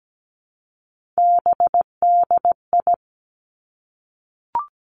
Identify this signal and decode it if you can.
Morse code, heard as a keyed tone, about 17 wpm, 700 Hz. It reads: BDI
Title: Three Challenging Letter-Number Combinations 17wpm